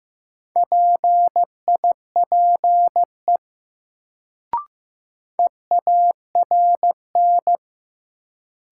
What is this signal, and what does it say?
Morse code, heard as a keyed tone, about 15 wpm, 700 Hz. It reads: PIPE EARN
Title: Sentences from Top 1500 Words - New Word Drill Alt - 15wpm